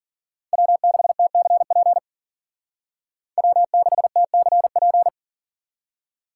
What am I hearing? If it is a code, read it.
Morse code, heard as a keyed tone, about 52 wpm, 700 Hz. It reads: W6TCP W6TCP